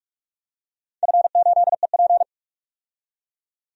Morse code, heard as a keyed tone, about 46 wpm, 700 Hz. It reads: F8EP
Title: DXCC-100 Call Signs - 4 Characters - 45wpm